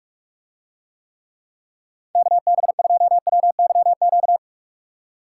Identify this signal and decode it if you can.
Morse code, heard as a keyed tone, about 44 wpm, 700 Hz. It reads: KB2WYQ